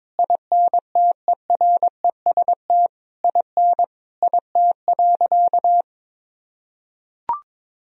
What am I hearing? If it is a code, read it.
Morse code, heard as a keyed tone, about 22 wpm, 700 Hz. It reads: INTEREST IN IT.
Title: Sentences from Top 1000 Words - Review Part 3 - 22wpm